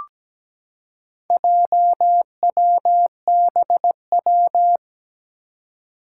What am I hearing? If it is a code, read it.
Morse code, heard as a keyed tone, about 17 wpm, 700 Hz. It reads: JWBW
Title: Four Challenging Letter-Number Combinations 17wpm